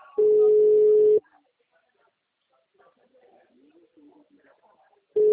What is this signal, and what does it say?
Ringback tone (Europe, 425 Hz), with speech.
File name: out- -2021